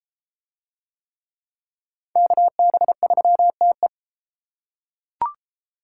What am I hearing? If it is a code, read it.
Morse code, heard as a keyed tone, about 33 wpm, 700 Hz. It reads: KB3TE